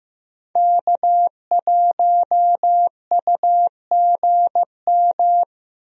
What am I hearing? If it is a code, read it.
Morse code, heard as a keyed tone, about 15 wpm, 700 Hz. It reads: K1UGM
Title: USA Call Signs - 5 Characters - 15wpm